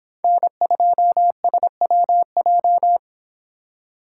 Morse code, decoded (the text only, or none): N2SWJ